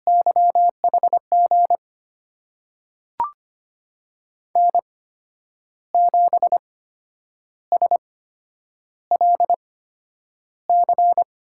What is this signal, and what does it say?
Morse code, heard as a keyed tone, about 25 wpm, spaced as for 9 wpm, 700 Hz. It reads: YHG N7SLC
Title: USA Call Signs - 5 Characters - 8x Character Spacing - 25wpm